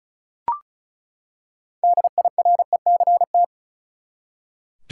Morse code, heard as a keyed tone, about 36 wpm, 700 Hz. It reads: DIRECT
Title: Six Letter Words - 35wpm